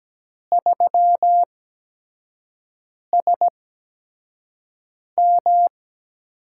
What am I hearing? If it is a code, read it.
Morse code, heard as a keyed tone, about 17 wpm, spaced as for 5 wpm, 700 Hz. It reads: 3SM
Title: USA Call Signs - 4 Characters - 8x Character Spacing - 17wpm